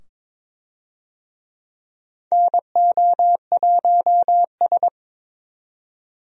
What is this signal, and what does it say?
Morse code, heard as a keyed tone, about 23 wpm, 700 Hz. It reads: NO1S
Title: USA Call Signs - 4 Characters - 22wpm